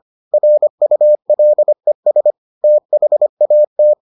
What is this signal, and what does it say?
Morse code, heard as a keyed tone, about 25 wpm, 600 Hz. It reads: RULES THAT